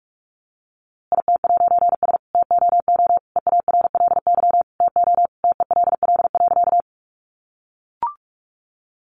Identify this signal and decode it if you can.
Morse code, heard as a keyed tone, about 44 wpm, 700 Hz. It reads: IT'S TOO EARLY TO TELL.
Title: A Sound Of Thunder 45wpm